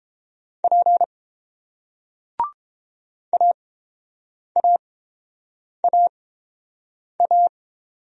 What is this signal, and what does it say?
Morse code, heard as a keyed tone, about 30 wpm, spaced as for 7 wpm, 700 Hz. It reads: P AAAA